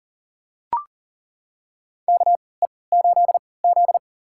Morse code, heard as a keyed tone, about 40 wpm, spaced as for 23 wpm, 700 Hz. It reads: KE8Z
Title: USA Call Signs - 3x Character Spacing - 40wpm